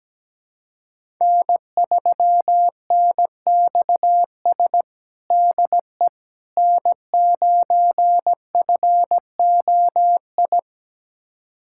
Morse code, heard as a keyed tone, about 17 wpm, 700 Hz. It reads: N3NXS DE N9FOI
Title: USA Callsign DE Callsign - 5 Characters - 17wpm